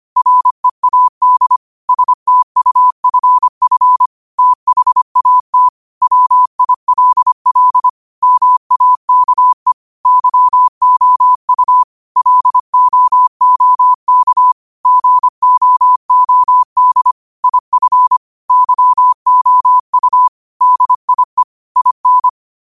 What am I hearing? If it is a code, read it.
Morse code, heard as a keyed tone, about 25 wpm, 1000 Hz. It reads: READ STUFF THAT WILL MAKE YOU LOOK GOOD IF YOU DIE IN
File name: qotdfiles / 25wpm